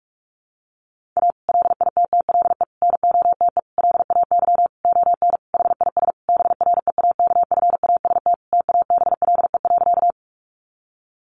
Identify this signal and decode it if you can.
Morse code, heard as a keyed tone, about 44 wpm, 700 Hz. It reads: A LITTLE NOTE LAY ON HIS BREAKFAST TABLE.
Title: Sentences from Top 1500 Words 45wpm